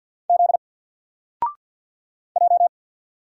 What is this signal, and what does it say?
Morse code, heard as a keyed tone, about 48 wpm, 700 Hz. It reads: Z J
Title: Single Character (Letter:Number:Punctuation:Prosign) 50wpm